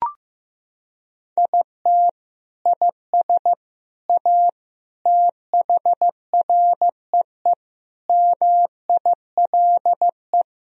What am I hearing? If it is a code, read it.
Morse code, heard as a keyed tone, about 15 wpm, 700 Hz. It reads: IT IS A THREE MILE
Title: Sentences from Top 1000 Words - Review Part 1 - 15wpm